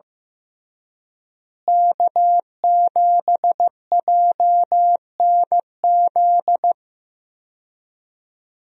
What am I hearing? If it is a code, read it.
Morse code, heard as a keyed tone, about 15 wpm, 700 Hz. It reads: K7JNZ
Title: USA Call Signs - 5 Characters - Repeated 3x - 15wpm